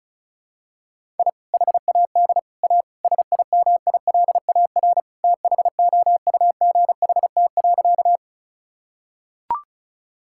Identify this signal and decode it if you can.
Morse code, heard as a keyed tone, about 36 wpm, 700 Hz. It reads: I HAD A SIMILAR THOUGHT.
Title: Sentences from Top 1000 Words - Review Part 5 - 35wpm